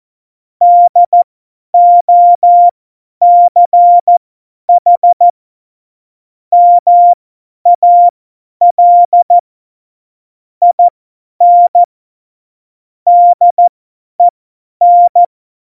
Morse code, heard as a keyed tone, about 14 wpm, spaced as for 10 wpm, 700 Hz. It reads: DOCH MAL IN DEN
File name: Morsekurs / Witze / Witz 25. September 2018, 50 F70